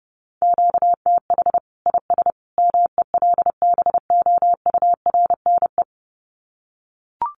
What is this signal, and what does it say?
Morse code, heard as a keyed tone, about 28 wpm, 700 Hz. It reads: QTH IS MELBOURNE
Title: Common QSO Phrases 30wpm